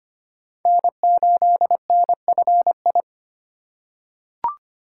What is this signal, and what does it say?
Morse code, heard as a keyed tone, about 25 wpm, 700 Hz. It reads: N8NFI
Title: USA Call Signs - 3x Character Spacing - 25wpm